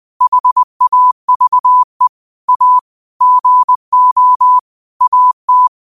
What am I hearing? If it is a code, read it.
Morse code, heard as a keyed tone, about 20 wpm, 1000 Hz. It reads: HAVE A GO AT